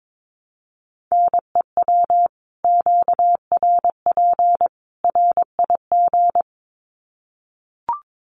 Morse code, heard as a keyed tone, about 22 wpm, 700 Hz. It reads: NEW QRP RIG
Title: Common QSO Phrases 22wpm